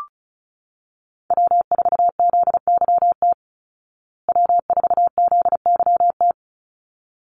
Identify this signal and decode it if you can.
Morse code, heard as a keyed tone, about 34 wpm, 700 Hz. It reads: W4ZYT W4ZYT
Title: Call Signs Repeated 3x - 35wpm